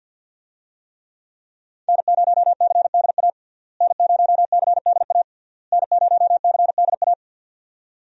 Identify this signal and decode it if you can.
Morse code, heard as a keyed tone, about 50 wpm, 700 Hz. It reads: N0CDA N0CDA N0CDA
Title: USA Call Signs - 5 Characters - Repeated 3x - 50wpm